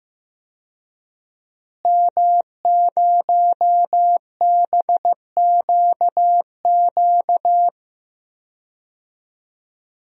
Morse code, heard as a keyed tone, about 15 wpm, 700 Hz. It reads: M0BQQ
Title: DXCC-100 Call Signs - 5 Characters - Repeated 2x - 15wpm